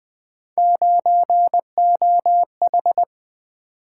Morse code, heard as a keyed tone, about 23 wpm, 700 Hz. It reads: 9OH
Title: Speed-Racing - Three Letter-Number Combinations - 20wpm